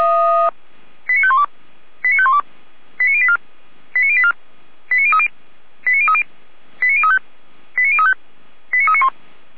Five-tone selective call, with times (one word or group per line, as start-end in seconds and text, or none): 1.1-1.4 89512
2.0-2.4 89512
3.0-3.4 89084
4.0-4.3 89084
4.9-5.3 89030
5.9-6.2 89030
6.8-7.2 89035
7.8-8.1 89035
8.7-9.1 89391